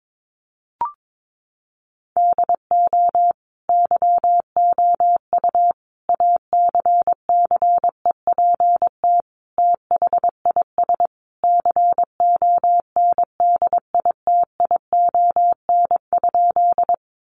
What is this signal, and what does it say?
Morse code, heard as a keyed tone, about 22 wpm, 700 Hz. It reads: DO YOU ACCEPT THIS CONDITION?
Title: Sentences from Top 1300 Words 22wpm